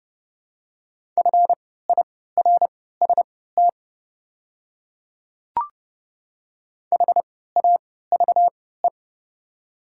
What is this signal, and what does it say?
Morse code, heard as a keyed tone, about 30 wpm, spaced as for 17 wpm, 700 Hz. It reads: FIRST HAVE
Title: 3x Character Spacing - Top 200 Words - 30wpm